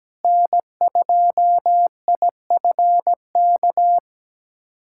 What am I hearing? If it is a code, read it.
Morse code, heard as a keyed tone, about 17 wpm, 700 Hz. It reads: N2IFK
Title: USA Callsign DE Callsign 17wpm